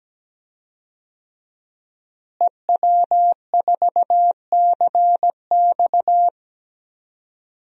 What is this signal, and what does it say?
Morse code, heard as a keyed tone, about 17 wpm, 700 Hz. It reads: EW4CX
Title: DXCC-100 Call Signs - 5 Characters - Repeated 2x - 17wpm